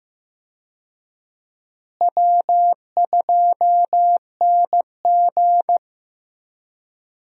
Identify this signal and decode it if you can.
Morse code, heard as a keyed tone, about 16 wpm, 700 Hz. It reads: W2NG
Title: USA Call Signs - 4 Characters - Repeated 3x - 15wpm